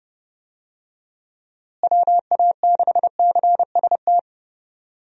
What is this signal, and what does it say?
Morse code, heard as a keyed tone, about 30 wpm, 700 Hz. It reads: WA6CST
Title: USA Call Signs - 6 Characters - Repeated 3x - 30wpm